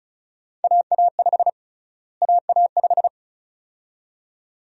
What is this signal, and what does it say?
Morse code, heard as a keyed tone, about 34 wpm, 700 Hz. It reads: AA5 AA5
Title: USA Call Sign Prefixes - Repeated 2x - 35wpm